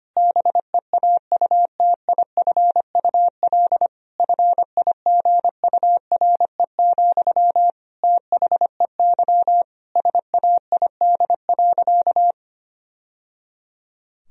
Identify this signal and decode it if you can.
Morse code, heard as a keyed tone, about 25 wpm, 700 Hz. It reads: BEAUTIFUL FIGURE, THEY SAID.
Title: Sentences from Top 1000 Words - Review Part 2 - 25wpm